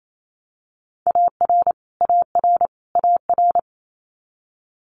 Morse code, heard as a keyed tone, about 28 wpm, 700 Hz. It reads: AR AR AR
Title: Multi-letter Phonemes Repeated 3x 28wpm